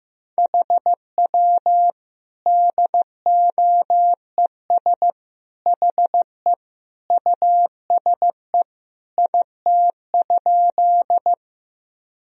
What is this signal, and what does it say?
Morse code, heard as a keyed tone, about 15 wpm, 700 Hz. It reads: HW DOES HE USE IT?
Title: Sentences from Top 200 Words and Abbreviations 15wpm